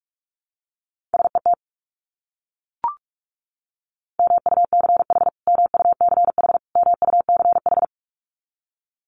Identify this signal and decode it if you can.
Morse code, heard as a keyed tone, about 44 wpm, 700 Hz. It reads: SET MUCH MUCH MUCH
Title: Top 200 Words Repeated 45wpm